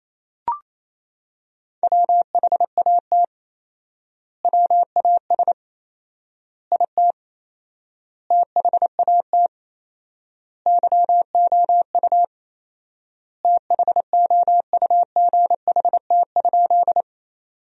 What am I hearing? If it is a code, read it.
Morse code, heard as a keyed tone, about 28 wpm, 700 Hz. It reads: WHAT WAS IT THAT YOU THOUGHT?
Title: 4x Word-Spacing - Sentences from Top 300 Words - 28wpm